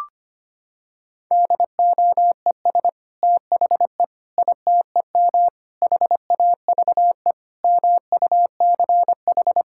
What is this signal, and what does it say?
Morse code, heard as a keyed tone, about 25 wpm, 700 Hz. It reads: DOES THE ITEM HAVE MUCH
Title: Sentences from Top 1000 Words - Review Part 6 - 25wpm